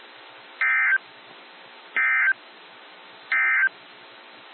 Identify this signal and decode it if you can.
Emergency Alert System (SAME) tones, heard as harsh buzzing bursts. It is heard from 0.5 to 3.8 s.
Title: NOAA Weather Radio - WXJ73